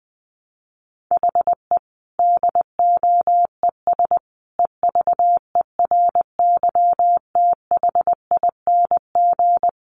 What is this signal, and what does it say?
Morse code, heard as a keyed tone, about 19 wpm, 700 Hz. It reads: HE DOES EVERYTHING
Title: Sentences from Top 900 Words 20wpm